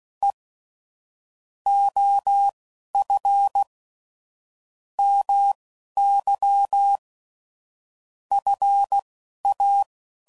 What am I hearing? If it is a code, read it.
Morse code, heard as a keyed tone, about 16 wpm, spaced as for 11 wpm, 800 Hz. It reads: E OF MY FA
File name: CW Code Groups @15WPM / QSOs / WD0FBT Spaced